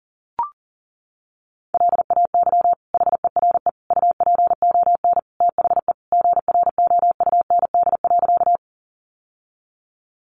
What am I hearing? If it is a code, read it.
Morse code, heard as a keyed tone, about 40 wpm, 700 Hz. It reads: LAY HERE UPON THE GROUND.
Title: Sentences from Top 1100 Words 40wpm